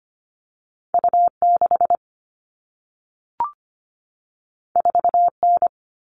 Morse code, heard as a keyed tone, about 25 wpm, 700 Hz. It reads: U 6 4 N
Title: All Cut Numbers - 2-Digit 25wpm